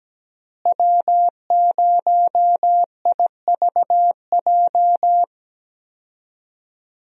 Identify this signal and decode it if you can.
Morse code, heard as a keyed tone, about 17 wpm, 700 Hz. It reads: W0IVJ
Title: USA Call Signs - 5 Characters - Repeated 2x - 17wpm